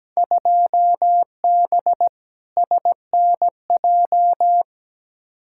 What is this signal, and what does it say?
Morse code, heard as a keyed tone, about 17 wpm, 700 Hz. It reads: 2B SNJ
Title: Field Day Exchanges Encore - 17wpm